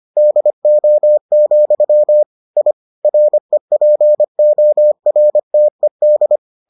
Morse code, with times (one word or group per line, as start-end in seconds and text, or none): 0.2-2.2 DO,
2.6-2.7 I
3.0-6.4 REPORTED